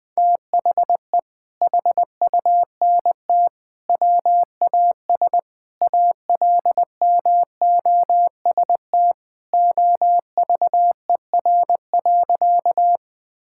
Morse code, heard as a keyed tone, about 21 wpm, 700 Hz. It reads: THE HUNT WAS ALMOST OVER.
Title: Sentences from Top 1600 Words 20wpm